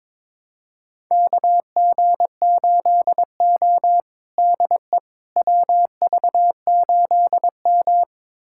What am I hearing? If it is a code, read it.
Morse code, heard as a keyed tone, about 22 wpm, 700 Hz. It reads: KG8O DE WV8M
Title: USA Callsign DE Callsign - 4 Characters - 22wpm